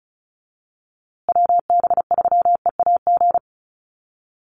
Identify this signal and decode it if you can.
Morse code, heard as a keyed tone, about 34 wpm, 700 Hz. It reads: WB3EAG
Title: USA Call Signs Repeated 3x 35wpm